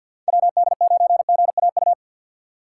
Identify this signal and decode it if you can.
Morse code, heard as a keyed tone, about 53 wpm, 700 Hz. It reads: WD9GAU
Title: USA Call Signs - 6 Characters - 8x Character Spacing - 50wpm